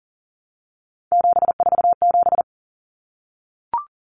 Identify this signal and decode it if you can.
Morse code, heard as a keyed tone, about 40 wpm, 700 Hz. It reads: 747
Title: Numbers - 3-Digit 40wpm